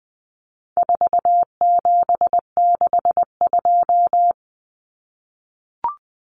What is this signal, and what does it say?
Morse code, heard as a keyed tone, about 20 wpm, 700 Hz. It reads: 4762
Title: Numbers - 4-Digit 20wpm